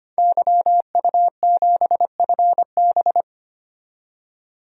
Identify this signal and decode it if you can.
Morse code, heard as a keyed tone, about 25 wpm, 700 Hz. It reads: YU7FB
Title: DXCC-100 Call Signs - 5 Characters - 25wpm